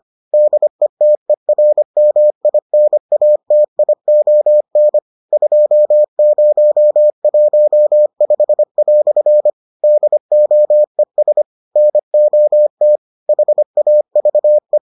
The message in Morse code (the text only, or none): DETERMINATION 2015" DOES NOT HAVE